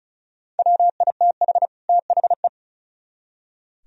Morse code, heard as a keyed tone, about 42 wpm, 700 Hz. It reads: WITH THE
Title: Speed-Racing - Sets of 2 Words 35wpm